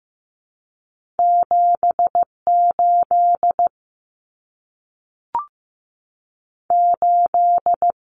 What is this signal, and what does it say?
Morse code, heard as a keyed tone, about 15 wpm, 700 Hz. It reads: 78 8
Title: Common Cut Numbers - 2-Digit 15wpm